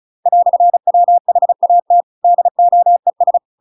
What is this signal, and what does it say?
Morse code, heard as a keyed tone, about 34 wpm, 700 Hz. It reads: "WHAT DOES
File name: Pinocchio / Pinocchio ch8 35 35 ik1hge